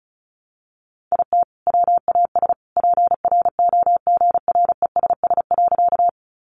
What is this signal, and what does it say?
Morse code, heard as a keyed tone, about 34 wpm, 700 Hz. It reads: IT WAS PROGRESS.
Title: Sentences from Top 1400 Words 35wpm